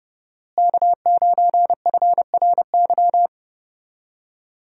Morse code, heard as a keyed tone, about 30 wpm, 700 Hz. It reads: K9FRY
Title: USA Call Signs - 5 Characters - Repeated 3x - 30wpm